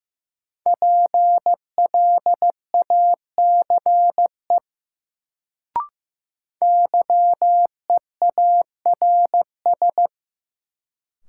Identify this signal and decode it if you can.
Morse code, heard as a keyed tone, about 15 wpm, 700 Hz. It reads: PLACE YEARS
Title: Sentences from Top 200 Words - New Word Drill - 15wpm